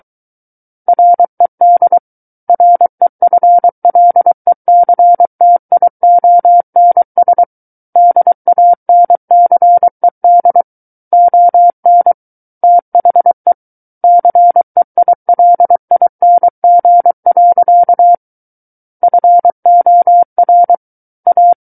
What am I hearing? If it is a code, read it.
Morse code, heard as a keyed tone, about 23 wpm, 700 Hz. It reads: RED REFLECTIONS DANCED ON THE CEILING. FOR A
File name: Book One / 14-Chapter 14 - 23 WPM